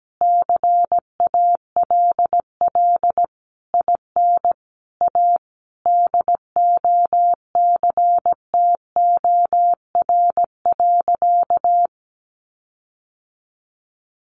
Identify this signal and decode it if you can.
Morse code, heard as a keyed tone, about 17 wpm, 700 Hz. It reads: CALL IN A DOCTOR.
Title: Sentences from Top 1500 Words 17wpm